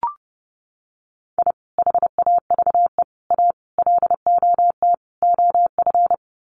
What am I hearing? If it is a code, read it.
Morse code, heard as a keyed tone, about 28 wpm, 700 Hz. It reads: I HAVE A LOT OF
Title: Sets of 5 Words 30wpm